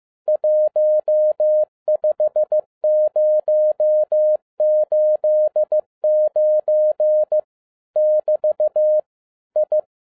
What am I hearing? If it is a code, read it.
Morse code, heard as a keyed tone, about 15 wpm, 610 Hz. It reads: 15089 <BT> I
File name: 2011-07-mp3 / n22 QRP OTAKU 15wpm